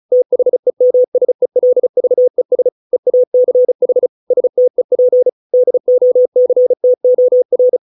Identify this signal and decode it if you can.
Morse code, heard as a keyed tone, about 34 wpm, 500 Hz. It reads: THEMSELVES EACH STEP DOCTOR